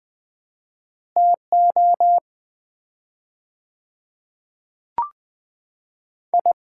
Morse code, heard as a keyed tone, about 20 wpm, 700 Hz. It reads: TO I
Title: 4x Word-Spacing - Sets of 5 Words - 20wpm